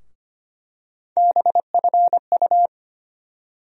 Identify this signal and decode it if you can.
Morse code, heard as a keyed tone, about 25 wpm, 700 Hz. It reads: BFU
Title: USA Call Sign Postfixes - 3 Letters 25wpm